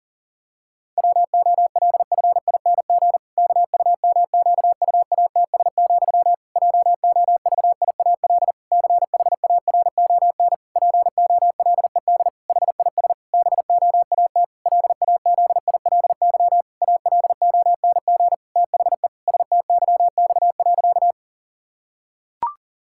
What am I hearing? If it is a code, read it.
Morse code, heard as a keyed tone, about 40 wpm, 700 Hz. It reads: WOLFING KUMQUATS, JOVIAL CHARON POLED HIS BOAT LAZILY ALONG THE STYX.
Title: Pangram Challenge v2 - 40wpm